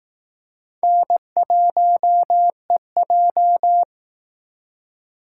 Morse code, heard as a keyed tone, about 18 wpm, 700 Hz. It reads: N1EJ
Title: Speed Racing - USA 1x2 Call Signs 15wpm